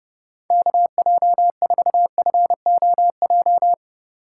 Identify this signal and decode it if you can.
Morse code, heard as a keyed tone, about 33 wpm, 700 Hz. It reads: KJ4FOJ
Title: Speed Racing - USA 2x3 Call Signs 30wpm